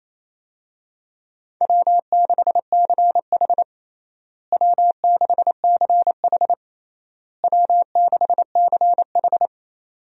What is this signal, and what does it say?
Morse code, heard as a keyed tone, about 28 wpm, 700 Hz. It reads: W6CH W6CH W6CH
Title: USA Call Signs - 4 Characters - Repeated 3x - 28wpm